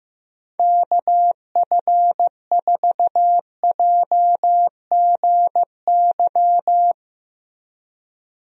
Morse code, heard as a keyed tone, about 15 wpm, 700 Hz. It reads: KF4JGY
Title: USA Call Signs - 6 Characters - Repeated 2x - 15wpm